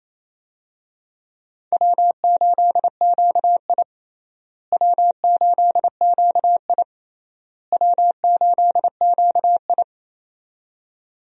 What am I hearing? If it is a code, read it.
Morse code, heard as a keyed tone, about 28 wpm, 700 Hz. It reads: W8QI W8QI W8QI
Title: USA Call Signs - 4 Characters - Repeated 3x - 28wpm